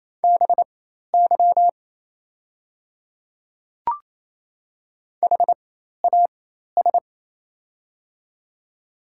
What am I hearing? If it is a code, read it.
Morse code, heard as a keyed tone, about 28 wpm, spaced as for 13 wpm, 700 Hz. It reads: BY HAS
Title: 4x Character Spacing - Top 100 Words - 28wpm